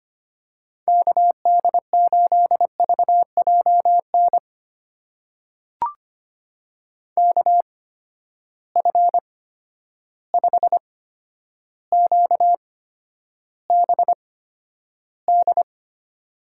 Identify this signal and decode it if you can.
Morse code, heard as a keyed tone, about 25 wpm, spaced as for 11 wpm, 700 Hz. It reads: KD8VJN KF5QBD
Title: USA Call Signs - 6 Characters - 8x Character Spacing - 25wpm